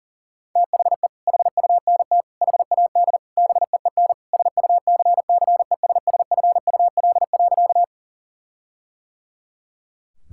Morse code, heard as a keyed tone, about 40 wpm, 700 Hz. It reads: THE HUNT HAD BEEN SUCCESSFUL.
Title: Sentences from Top 1600 Words 40wpm